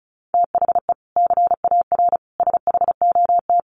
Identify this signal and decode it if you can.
Morse code, heard as a keyed tone, about 34 wpm, 700 Hz. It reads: THE CAR SHOT
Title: Sentences from Top 1300 Words 35wpm